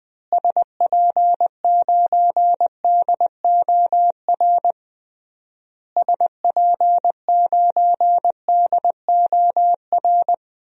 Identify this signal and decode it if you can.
Morse code, heard as a keyed tone, about 21 wpm, 700 Hz. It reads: SP9DOR SP9DOR
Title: DXCC-100 Call Signs - 6 Characters - Repeated 3x - 20wpm